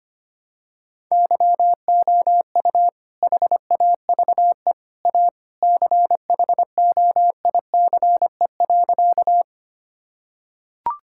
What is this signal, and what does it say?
Morse code, heard as a keyed tone, about 25 wpm, 700 Hz. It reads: YOU HAVE A CHOICE.
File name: practice / 25wpm / Sentences with 3-Word Phrases - 25wpm